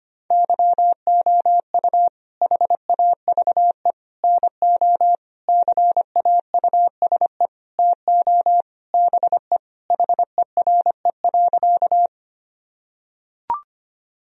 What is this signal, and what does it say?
Morse code, heard as a keyed tone, about 25 wpm, 700 Hz. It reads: YOU HAVE NO CAUSE TO BE HERE.
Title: Sentences from Top 1000 Words - Review Part 1 - 25wpm